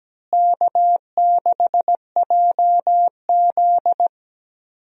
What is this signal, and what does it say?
Morse code, heard as a keyed tone, about 17 wpm, 700 Hz. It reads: K6JZ
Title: USA Call Signs - 4 Characters - Repeated 2x - 17wpm